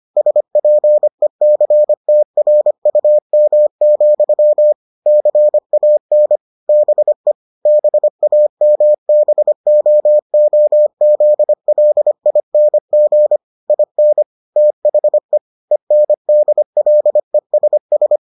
Morse code, heard as a keyed tone, about 25 wpm, 600 Hz. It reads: SPECTRUM, CAN BE BAMBOOZLING IN THE ENDLESS